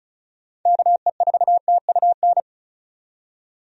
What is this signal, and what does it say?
Morse code, heard as a keyed tone, about 34 wpm, 700 Hz. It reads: KE4TUN
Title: USA Call Signs - 6 Characters - 35wpm